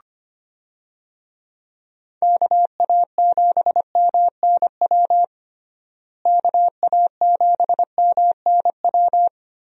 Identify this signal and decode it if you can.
Morse code, heard as a keyed tone, about 25 wpm, 700 Hz. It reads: KA7MNW KA7MNW
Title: USA Call Signs - 6 Characters - Repeated 3x - 25wpm